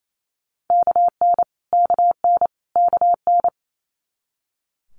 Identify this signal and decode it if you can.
Morse code, heard as a keyed tone, about 28 wpm, 700 Hz. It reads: KN KN KN
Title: Multi-letter Phonemes Repeated 3x 28wpm